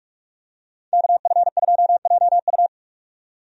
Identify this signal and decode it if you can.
Morse code, heard as a keyed tone, about 46 wpm, 700 Hz. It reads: KF2JU
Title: USA Call Signs - 5 Characters - 45wpm